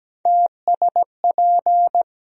Morse code, heard as a keyed tone, about 17 wpm, 700 Hz. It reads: TSP
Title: USA Call Sign Postfixes - 3 Letters 17wpm